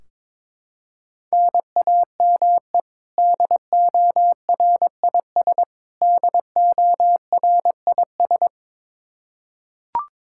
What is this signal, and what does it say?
Morse code, heard as a keyed tone, about 22 wpm, 700 Hz. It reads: NAME DORIS DORIS
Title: Practice QSO - Exchange 1 - Beginning - 22wpm